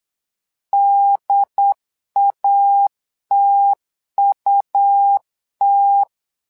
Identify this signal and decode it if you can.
Morse code, heard as a keyed tone, about 8 wpm, 800 Hz. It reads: DATUT
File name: LEZIONI / 2 UHDN / 2 UHDN 8wpm2232